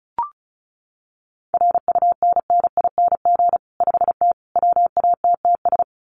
Morse code, heard as a keyed tone, about 34 wpm, 700 Hz. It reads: RUNNING 5T WATTS
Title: Common QSO Phrases 35wpm